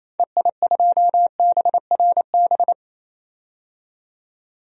Morse code, heard as a keyed tone, about 28 wpm, 700 Hz. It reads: EI2BRB
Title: DXCC-100 Call Signs - 6 Characters - 28wpm